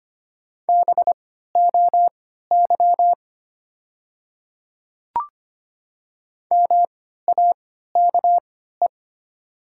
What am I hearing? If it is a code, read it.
Morse code, heard as a keyed tone, about 25 wpm, spaced as for 14 wpm, 700 Hz. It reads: BOY MAKE
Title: 3x Character Spacing - Top 200 Words - 25wpm